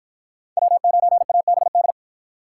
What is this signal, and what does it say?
Morse code, heard as a keyed tone, about 50 wpm, 700 Hz. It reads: W9ABD